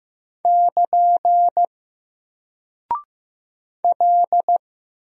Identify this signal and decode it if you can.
Morse code, heard as a keyed tone, about 15 wpm, 700 Hz. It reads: <KN> L
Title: Single Character (Letter:Number:Punctuation:Prosign) Alt 15wpm